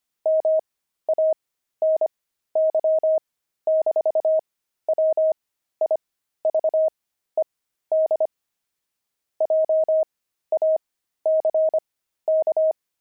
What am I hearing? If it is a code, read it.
Morse code, heard as a keyed tone, about 25 wpm, spaced as for 13 wpm, 630 Hz. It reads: MANY-WIVED JACK
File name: pangram-fs-13-cs-25-wived